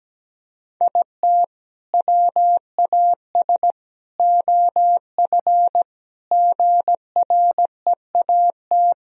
Morse code, heard as a keyed tone, about 17 wpm, 700 Hz. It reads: IT WAS OF GREAT
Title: Sentences from Top 1000 Words - Review Part 2 - 17wpm